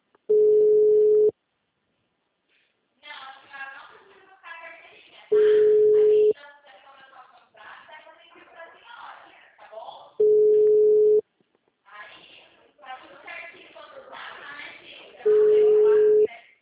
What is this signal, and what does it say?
Ringback tone (Europe, 425 Hz), with speech.